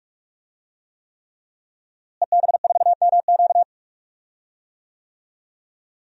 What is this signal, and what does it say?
Morse code, heard as a keyed tone, about 46 wpm, 700 Hz. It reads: EB4MQ
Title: DXCC-100 Call Signs - 5 Characters - Repeated 3x - 45wpm